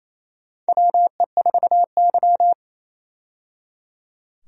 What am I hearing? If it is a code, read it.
Morse code, heard as a keyed tone, about 28 wpm, 700 Hz. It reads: WE4Y
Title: USA Call Signs - 4 Characters - 28wpm